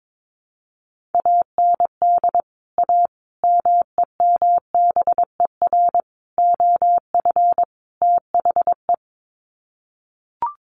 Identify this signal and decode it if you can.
Morse code, heard as a keyed tone, about 22 wpm, 700 Hz. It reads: AND A MEMBER OF THE
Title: Sets of 5 Words 22wpm